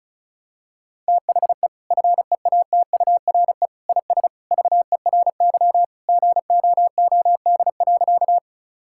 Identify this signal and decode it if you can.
Morse code, heard as a keyed tone, about 36 wpm, 700 Hz. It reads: THE FEATURE IS VERY GOOD.
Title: Sentences from Top 1000 Words - Review Part 5 - 35wpm